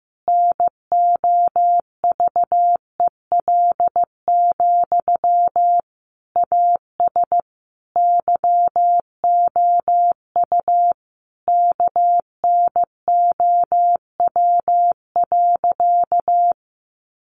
Morse code, heard as a keyed tone, about 14 wpm, 700 Hz. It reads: NOVEL, AS YOU KNOW.
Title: Sentences from Top 1300 Words 15wpm